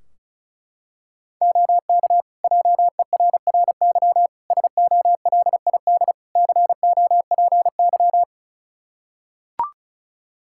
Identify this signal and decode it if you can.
Morse code, heard as a keyed tone, about 36 wpm, 700 Hz. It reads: OK JERRY SOLID COPY
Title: Practice QSO - Exchange 3 - More Info - 35wpm